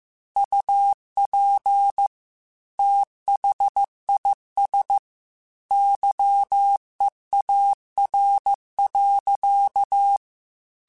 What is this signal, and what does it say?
Morse code, heard as a keyed tone, about 15 wpm, 800 Hz. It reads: UP THIS YEAR.